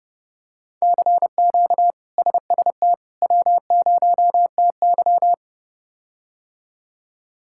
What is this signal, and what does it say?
Morse code, heard as a keyed tone, about 30 wpm, 700 Hz. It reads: CQ SST W0TY
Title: K1USN SST QSO Practice - 30wpm